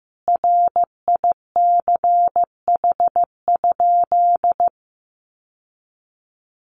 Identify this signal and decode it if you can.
Morse code, heard as a keyed tone, about 15 wpm, 700 Hz. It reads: RICH?
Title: Sentences from Top 1100 Words 15wpm